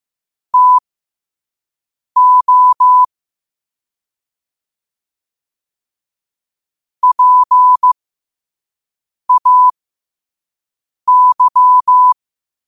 Morse code, heard as a keyed tone, about 15 wpm, spaced as for 5 wpm, 1000 Hz. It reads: TO PAY
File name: qotdfiles / 5wpm